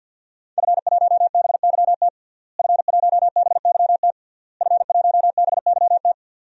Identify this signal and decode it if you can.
Morse code, heard as a keyed tone, about 51 wpm, 700 Hz. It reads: F1BYT F1BYT F1BYT
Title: DXCC-100 Call Signs - 5 Characters - Repeated 3x - 50wpm